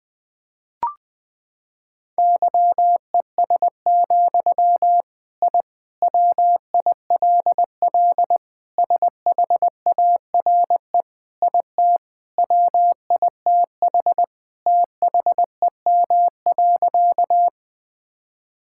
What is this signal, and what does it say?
Morse code, heard as a keyed tone, about 20 wpm, 700 Hz. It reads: YES, I WILL SHARE IT WITH THEM.
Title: Sentences from Top 1000 Words - Review Part 6 - 20wpm